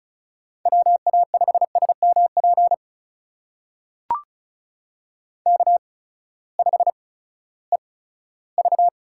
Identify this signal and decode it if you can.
Morse code, heard as a keyed tone, about 36 wpm, spaced as for 18 wpm, 700 Hz. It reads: WA5SMP K5EV
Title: USA Call Signs - 8x Character Spacing - 35wpm